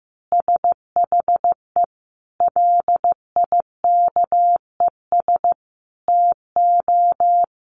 Morse code, heard as a keyed tone, about 14 wpm, 700 Hz. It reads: SHE LIKES TO
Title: Easy Phrases 15wpm